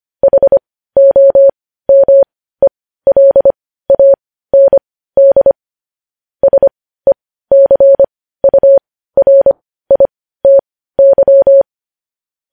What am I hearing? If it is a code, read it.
Morse code, heard as a keyed tone, about 25 wpm, spaced as for 15 wpm, 550 Hz. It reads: HOMELAND SECURITY
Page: news-headlines-in-morse-code-at-15-wpm